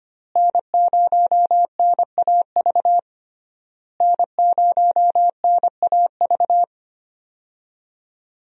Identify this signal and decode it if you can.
Morse code, heard as a keyed tone, about 25 wpm, 700 Hz. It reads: N0NAV N0NAV
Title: USA Call Signs - 5 Characters - Repeated 3x - 25wpm